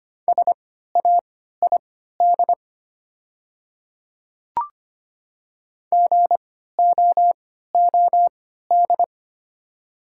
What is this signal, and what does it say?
Morse code, heard as a keyed tone, about 25 wpm, spaced as for 14 wpm, 700 Hz. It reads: SAID GOOD
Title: 3x Character Spacing - Top 200 Words - 25wpm